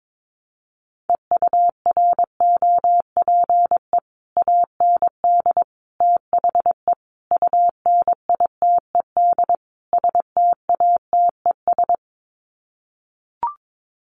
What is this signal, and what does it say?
Morse code, heard as a keyed tone, about 22 wpm, 700 Hz. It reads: EUROPE AND THE UNITED STATES
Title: Sets of 5 Words Encore2 22wpm